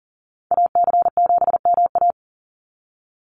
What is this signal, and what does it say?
Morse code, heard as a keyed tone, about 40 wpm, 700 Hz. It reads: AC7MA
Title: NA SOTA Activator Callsigns 40wpm